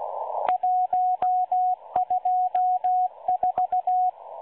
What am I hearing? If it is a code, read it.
Morse code, heard as a keyed tone, about 16 wpm, 730 Hz, with radio noise. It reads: 124
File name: M12-Morse-Numbers-Station-8116-kHz-CW